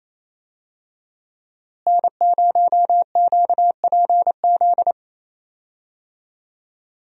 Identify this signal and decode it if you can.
Morse code, heard as a keyed tone, about 28 wpm, 700 Hz. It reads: N0QPZ
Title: USA Call Signs - 5 Characters - Repeated 3x - 28wpm